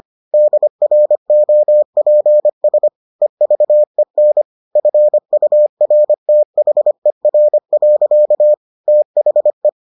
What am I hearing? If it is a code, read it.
Morse code, heard as a keyed tone, about 25 wpm, 600 Hz. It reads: DROPS EVEN FURTHER. THE